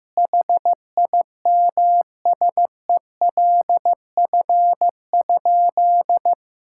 Morse code, heard as a keyed tone, about 15 wpm, 700 Hz. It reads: HIMSELF?
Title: Sentences from Top 1600 Words 15wpm